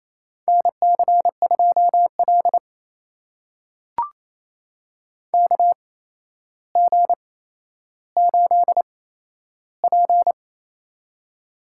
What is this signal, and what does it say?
Morse code, heard as a keyed tone, about 28 wpm, spaced as for 12 wpm, 700 Hz. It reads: NC2L KG8P
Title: USA Call Signs - 4 Characters - 8x Character Spacing - 28wpm